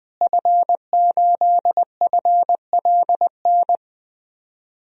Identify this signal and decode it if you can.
Morse code, heard as a keyed tone, about 20 wpm, 700 Hz. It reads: F8FLN
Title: DXCC-100 Call Signs - 5 Characters - Repeated 2x - 20wpm